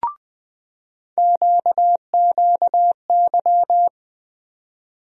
Morse code, heard as a keyed tone, about 20 wpm, 700 Hz. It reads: QQY
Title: USA Call Sign Postfixes - 3 Letters 20wpm